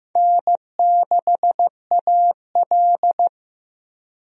Morse code, heard as a keyed tone, about 15 wpm, 700 Hz. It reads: N6AL
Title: USA Call Signs - 4 Characters - 15wpm